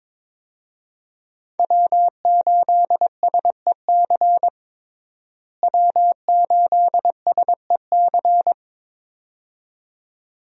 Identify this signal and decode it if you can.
Morse code, heard as a keyed tone, about 22 wpm, 700 Hz. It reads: W8SEC W8SEC
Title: USA Call Signs - 5 Characters - Repeated 2x - 22wpm